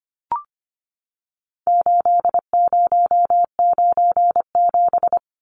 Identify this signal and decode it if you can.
Morse code, heard as a keyed tone, about 25 wpm, 700 Hz. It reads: 8097
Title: Numbers - 4-Digit 25wpm